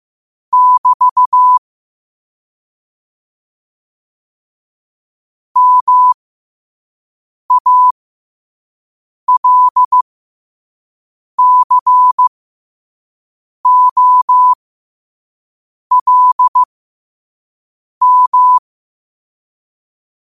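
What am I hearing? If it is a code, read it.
Morse code, heard as a keyed tone, about 15 wpm, spaced as for 5 wpm, 1000 Hz. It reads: <BT> MALCOLM
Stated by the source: Morse code sent at 5 wpm for learning.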